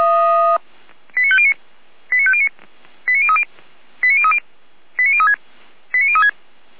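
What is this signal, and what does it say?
Five-tone selective call (ZVEI), heard as quick runs of five tones, tones of about 70 ms. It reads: 89509 89509 89030 89030 89036 89036
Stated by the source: ZVEI five-tone selective call received on an SDR.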